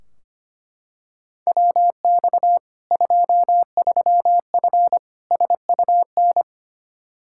Morse code, heard as a keyed tone, about 25 wpm, 700 Hz. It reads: WX 23F SUN
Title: Practice QSO - Exchange 2 - Info - 25wpm